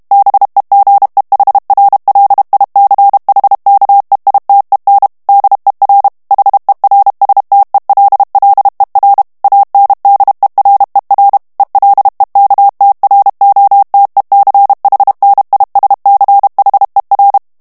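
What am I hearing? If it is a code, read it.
Morse code, heard as a keyed tone, about 32 wpm, 800 Hz. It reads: BEGEHRLICHKEITEN DER HERSTELLER ANDERER ELEKTROTECHNISCHER
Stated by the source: German plain-text Morse reading practice at 30 words per minute.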